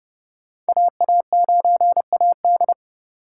Morse code, heard as a keyed tone, about 39 wpm, 700 Hz. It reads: AA9AD